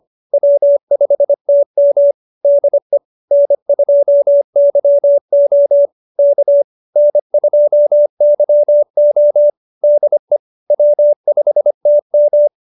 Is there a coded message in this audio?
Yes: Morse code from 0.3 to 12.5 s, heard as a keyed tone, about 25 wpm, 590 Hz. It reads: W5TM DE N2YO K N2YO DE W5TM